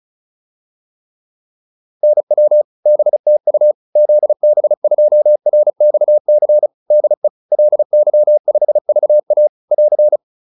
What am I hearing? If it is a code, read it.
Morse code, heard as a keyed tone, about 34 wpm, 600 Hz. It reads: NW BTU ZB2RXC DE LY5VA <AR>